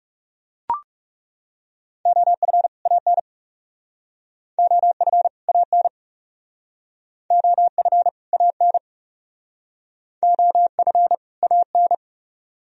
Morse code, heard as a keyed tone, about 36 wpm, 700 Hz. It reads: OF AN OF AN OF AN OF AN
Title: Speed-Racing - Sets of 2 Words 30wpm